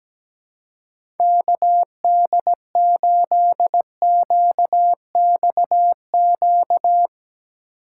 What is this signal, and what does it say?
Morse code, heard as a keyed tone, about 17 wpm, 700 Hz. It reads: KD8QXQ
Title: USA Call Signs - 6 Characters - 2x Character Spacing - 17wpm